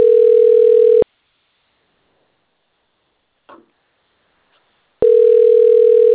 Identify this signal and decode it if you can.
Ringback tone (North America), 440 + 480 Hz.